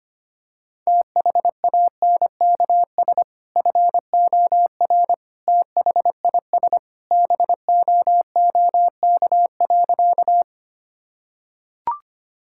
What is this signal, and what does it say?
Morse code, heard as a keyed tone, about 25 wpm, 700 Hz. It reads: THANKS FOR THIS BOOK.
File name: practice / 25wpm / Sentences with 2-Word Phrases - 25wpm